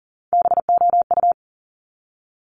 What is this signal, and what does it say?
Morse code, heard as a keyed tone, about 42 wpm, 700 Hz. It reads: BOU
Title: Top Three Letter Combinations in Words 40wpm